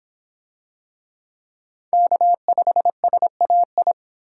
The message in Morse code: K5SAI

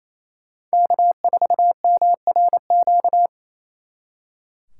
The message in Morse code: K4MRQ

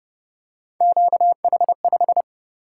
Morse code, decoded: QH5

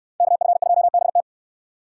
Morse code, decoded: KF2BT

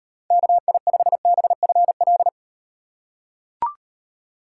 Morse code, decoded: KI5BFL

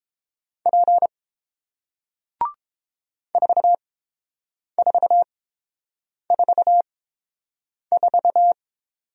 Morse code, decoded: P 4444

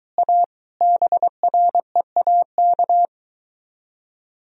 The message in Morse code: A BREAK

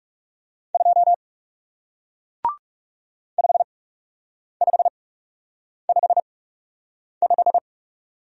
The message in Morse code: 2 5555

